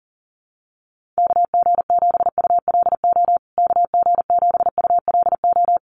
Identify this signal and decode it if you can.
Morse code, heard as a keyed tone, about 40 wpm, 700 Hz. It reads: KG7ULO KG7ULO
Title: USA Call Signs Repeated 3x 40wpm